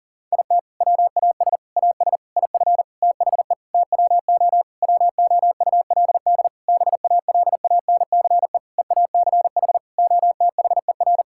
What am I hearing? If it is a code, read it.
Morse code, heard as a keyed tone, about 40 wpm, 700 Hz. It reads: IT WAS AS IF THE TWO WOULD BALANCE EACH OTHER
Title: Sentences from Top 1000 Words - Review Part 6 - 40wpm